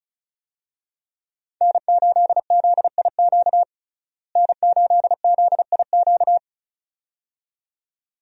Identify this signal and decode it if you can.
Morse code, heard as a keyed tone, about 36 wpm, 700 Hz. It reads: N8ZIQ N8ZIQ
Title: USA Call Signs - 5 Characters - Repeated 2x - 35wpm